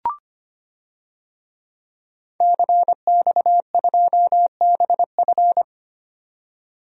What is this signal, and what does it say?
Morse code, heard as a keyed tone, about 25 wpm, 690 Hz. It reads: CX2BF
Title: DXCC-100 Call Signs - 5 Characters - 25wpm